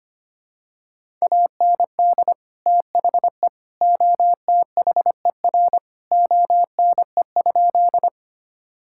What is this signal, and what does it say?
Morse code, heard as a keyed tone, about 25 wpm, 700 Hz. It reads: AND THE OTHER ONE?
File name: practice / 25wpm / Sentences with 3-Word Phrases - 25wpm